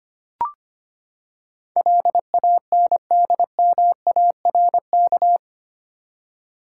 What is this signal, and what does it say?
Morse code, heard as a keyed tone, about 25 wpm, 700 Hz. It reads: LANDMARK
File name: Eight Letter Words - 25wpm